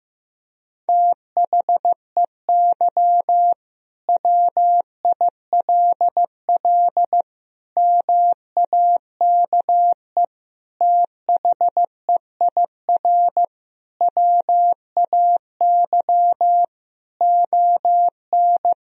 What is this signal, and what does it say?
Morse code, heard as a keyed tone, about 15 wpm, 700 Hz. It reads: THEY WILL MAKE THEIR WAY ON
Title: Sentences from Top 1000 Words - Review Part 3 - 15wpm